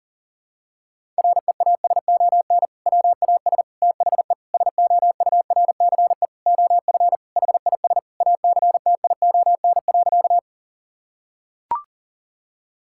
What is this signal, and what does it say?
Morse code, heard as a keyed tone, about 37 wpm, 700 Hz. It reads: REASON WAS THE SOURCE OF HIS ACTION.
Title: Sentences from Top 1000 Words - Review Part 5 - 40wpm